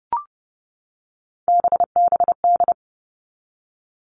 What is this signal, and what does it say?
Morse code, heard as a keyed tone, about 28 wpm, 700 Hz. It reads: BBD
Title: All Cut Numbers - 3-Digit 30wpm